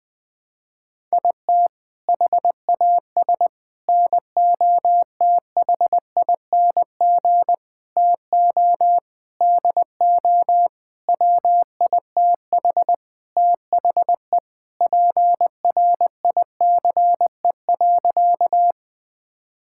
Morse code, heard as a keyed tone, about 20 wpm, 700 Hz. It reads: IT HAS NOTHING TO DO WITH THE PRICE.
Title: Sentences with 7-Word Phrases - Spoken Beginning - 20wpm